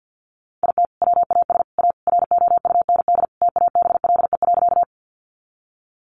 Morse code, heard as a keyed tone, about 50 wpm, 700 Hz. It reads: IT WAS A ROUND TABLE.
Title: Sentences from Top 400 Words 50wpm